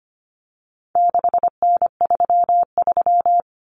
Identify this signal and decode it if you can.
Morse code, heard as a keyed tone, about 25 wpm, 700 Hz. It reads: 6N33
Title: Common Cut Numbers - 4-Digit 25wpm